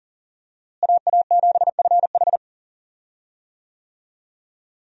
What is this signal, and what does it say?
Morse code, heard as a keyed tone, about 40 wpm, 700 Hz. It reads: AA7FH
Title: USA Call Signs - 5 Characters - Repeated 3x - 40wpm